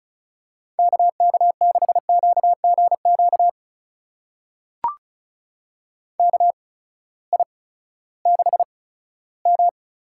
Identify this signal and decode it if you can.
Morse code, heard as a keyed tone, about 36 wpm, spaced as for 18 wpm, 700 Hz. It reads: KK6QGQ KI6M
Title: USA Call Signs - 6 Characters - 8x Character Spacing - 35wpm